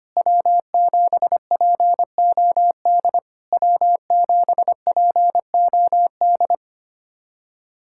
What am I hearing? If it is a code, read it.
Morse code, heard as a keyed tone, about 25 wpm, 690 Hz. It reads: W7POD W7POD
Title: SOTA QSO Practice 25wpm